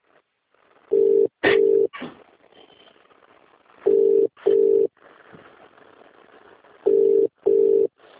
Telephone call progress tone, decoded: ringback tone